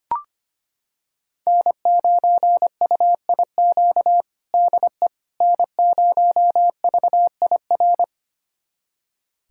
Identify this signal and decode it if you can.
Morse code, heard as a keyed tone, about 25 wpm, 700 Hz. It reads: N9UIQ DE N0VIR
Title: USA Callsign DE Callsign - 5 Characters - 25wpm